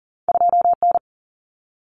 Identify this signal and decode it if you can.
Morse code, heard as a keyed tone, about 38 wpm, 700 Hz. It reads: 2N
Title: Common Cut Numbers - 2-Digit 40wpm